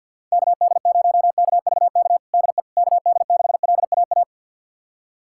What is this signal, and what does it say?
Morse code, heard as a keyed tone, about 47 wpm, 700 Hz. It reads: KD0KVK DE KD6LAA
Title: USA Callsign DE Callsign - 6 Characters - 50wpm